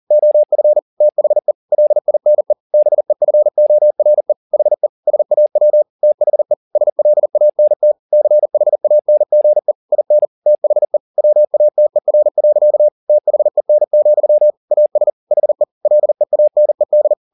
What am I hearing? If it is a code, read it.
Morse code, heard as a keyed tone, about 40 wpm, 600 Hz. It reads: OF THE LINE BEFORE HE SAW THE SLANT CHANGE IN THE WATER. THEN, AS HE LEANED